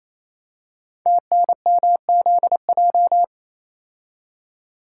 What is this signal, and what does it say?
Morse code, heard as a keyed tone, about 28 wpm, 700 Hz. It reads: TNMZJ
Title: Code Groups - 5 Letters - 28wpm